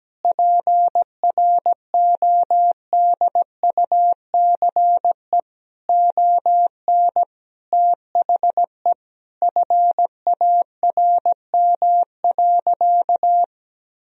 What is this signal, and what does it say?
Morse code, heard as a keyed tone, about 17 wpm, 700 Hz. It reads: PRODUCE ON THE FARM.
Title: Sentences from Top 1000 Words - Review Part 3 - 17wpm